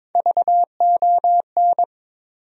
Morse code, heard as a keyed tone, about 22 wpm, 700 Hz. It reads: VON
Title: USA Call Sign Postfixes - 3 Letters 22wpm